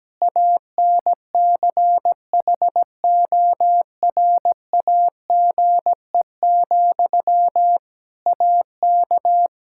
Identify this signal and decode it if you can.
Morse code, heard as a keyed tone, about 17 wpm, 700 Hz. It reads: ANCHORAGE, AK
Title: K1USN SST QSO Practice 17wpm